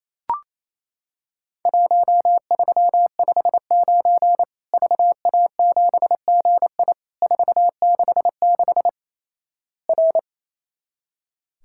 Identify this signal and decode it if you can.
Morse code, heard as a keyed tone, about 27 wpm, 700 Hz. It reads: 1359 VA7GI 466 R
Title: Worked All Europe DX Contest - QTC Exchange - 28wpm